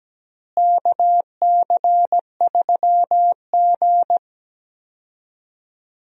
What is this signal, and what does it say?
Morse code, heard as a keyed tone, about 17 wpm, 700 Hz. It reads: KC3G
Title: USA Call Signs - 4 Characters - Repeated 3x - 17wpm